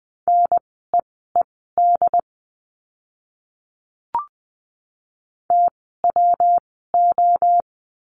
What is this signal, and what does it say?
Morse code, heard as a keyed tone, about 20 wpm, spaced as for 14 wpm, 700 Hz. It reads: NEED TWO
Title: 139 Lesson 23f - Letter f - Words 20wpm